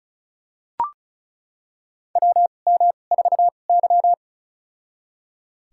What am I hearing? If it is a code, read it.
Morse code, heard as a keyed tone, about 36 wpm, spaced as for 25 wpm, 700 Hz. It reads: WM4Y
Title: USA Call Signs - 4 Characters - 2x Character Spacing - 35wpm